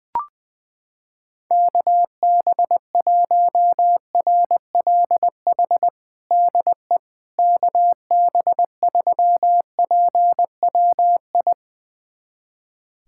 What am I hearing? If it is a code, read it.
Morse code, heard as a keyed tone, about 21 wpm, 700 Hz. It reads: KB1RLH DE KB3PWI
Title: USA Callsign DE Callsign 20wpm